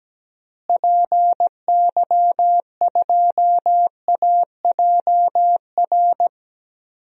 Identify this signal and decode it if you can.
Morse code, heard as a keyed tone, about 17 wpm, 700 Hz. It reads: PY2AJR
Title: DXCC-100 Call Signs - 6 Characters - Repeated 3x - 17wpm